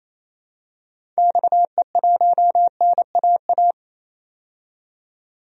Morse code, heard as a keyed tone, about 28 wpm, 700 Hz. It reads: XE1NAA
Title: DXCC-100 Call Signs - 6 Characters - 28wpm